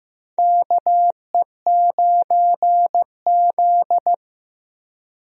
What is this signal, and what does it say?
Morse code, heard as a keyed tone, about 15 wpm, 700 Hz. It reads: KE9Z
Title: USA Call Signs - 4 Characters - Repeated 2x - 15wpm